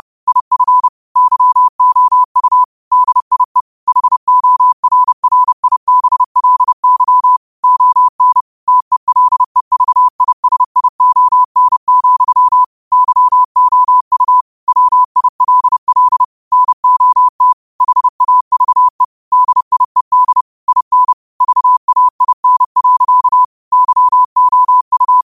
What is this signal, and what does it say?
Morse code, heard as a keyed tone, about 30 wpm, 1000 Hz. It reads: IF YOU DIE HORRIBLY ON TELEVISION, YOU WILL NOT HAVE DIED IN VAIN. YOU
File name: qotdfiles / 30wpm